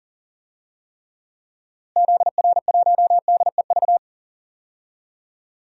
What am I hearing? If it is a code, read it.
Morse code, heard as a keyed tone, about 40 wpm, 700 Hz. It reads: ZR1DEV
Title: DXCC-100 Call Signs - 6 Characters - Repeated 3x - 40wpm